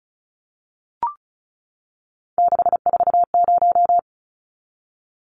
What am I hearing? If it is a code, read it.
Morse code, heard as a keyed tone, about 34 wpm, 700 Hz. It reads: 640
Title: Numbers - 3-Digit 35wpm